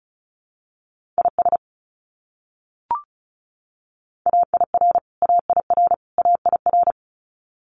Morse code, heard as a keyed tone, about 34 wpm, 700 Hz. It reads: IS AIR AIR AIR
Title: Top 300 Words Repeated 35wpm